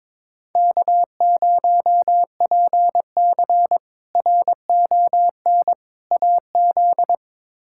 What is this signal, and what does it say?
Morse code, heard as a keyed tone, about 22 wpm, 700 Hz. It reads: K0PC RON AZ
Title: K1USN SST QSO Practice - 22wpm